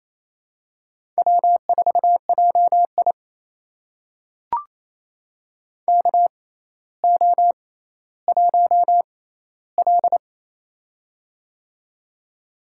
Morse code, heard as a keyed tone, about 28 wpm, spaced as for 14 wpm, 700 Hz. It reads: W4JI KO1L
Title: USA Call Signs - 4 Characters - 6x Character Spacing - 28wpm